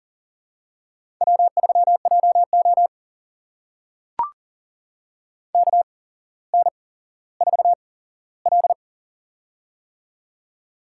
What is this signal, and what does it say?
Morse code, heard as a keyed tone, about 40 wpm, spaced as for 17 wpm, 700 Hz. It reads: W3JO KN4L
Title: USA Call Signs - 4 Characters - 8x Character Spacing - 40wpm